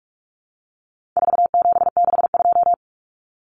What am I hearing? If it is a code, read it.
Morse code, heard as a keyed tone, about 44 wpm, 700 Hz. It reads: 4762
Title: Numbers - 4-Digit 45wpm